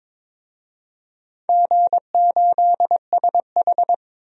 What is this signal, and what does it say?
Morse code, heard as a keyed tone, about 22 wpm, 700 Hz. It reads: G8SH
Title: DXCC-100 Call Signs - 4 Characters - 22wpm